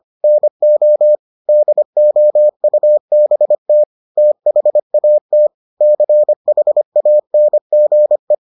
Morse code, heard as a keyed tone, about 25 wpm, 600 Hz. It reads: NO DOUBT THAT CHANGE